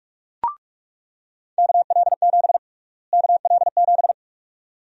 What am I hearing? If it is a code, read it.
Morse code, heard as a keyed tone, about 46 wpm, 700 Hz. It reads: KL7 KL7
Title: USA Call Sign Prefixes - Repeated 2x - 45wpm